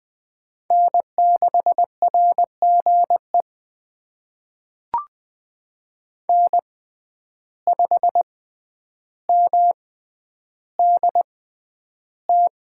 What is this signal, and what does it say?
Morse code, heard as a keyed tone, about 21 wpm, spaced as for 10 wpm, 700 Hz. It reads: N6RGE N5MDT
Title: USA Call Signs - 6x Character Spacing - 20wpm